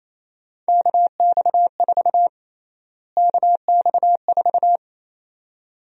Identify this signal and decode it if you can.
Morse code, heard as a keyed tone, about 28 wpm, 700 Hz. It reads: KX4 KX4
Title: USA Call Sign Prefixes - Repeated 2x - 28wpm